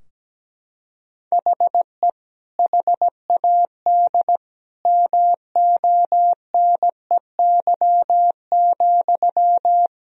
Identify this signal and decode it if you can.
Morse code, heard as a keyed tone, about 17 wpm, 700 Hz. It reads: HE HAD MONEY,
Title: Sentences from Top 1000 Words - Review Part 3 - 17wpm